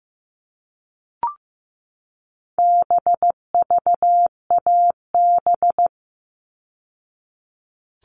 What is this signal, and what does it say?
Morse code, heard as a keyed tone, about 15 wpm, 700 Hz. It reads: BVAB
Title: All Cut Numbers - 4-Digit 15wpm